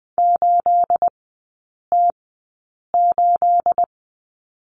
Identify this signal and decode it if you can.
Morse code, heard as a keyed tone, about 20 wpm, spaced as for 8 wpm, 690 Hz. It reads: 8T8
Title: 212 Lesson 34e - Number 8 - Three Characters 20wpm